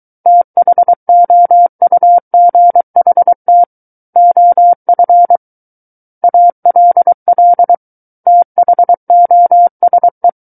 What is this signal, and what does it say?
Morse code, heard as a keyed tone, about 23 wpm, 700 Hz. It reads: THOUGHT OF ALL THOSE
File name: Book One / 14-Chapter 14 - 23 WPM